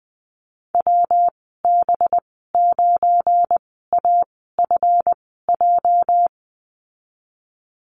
Morse code, heard as a keyed tone, about 19 wpm, spaced as for 14 wpm, 700 Hz. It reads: WB9AFJ
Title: 219 Lesson 35g - Number 0 - Call Signs 20wpm